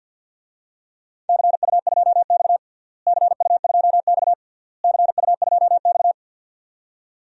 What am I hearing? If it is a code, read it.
Morse code, heard as a keyed tone, about 52 wpm, 700 Hz. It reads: CU2X CU2X CU2X
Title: DXCC-100 Call Signs - 4 Characters - Repeated 3x - 50wpm